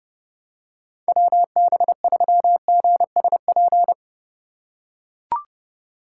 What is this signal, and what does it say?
Morse code, heard as a keyed tone, about 29 wpm, 700 Hz. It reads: WB3GSP